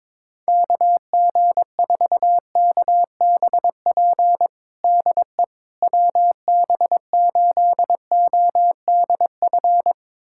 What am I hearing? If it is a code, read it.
Morse code, heard as a keyed tone, about 22 wpm, 700 Hz. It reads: KG4KBP DE WB8ODF
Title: USA Callsign DE Callsign 22wpm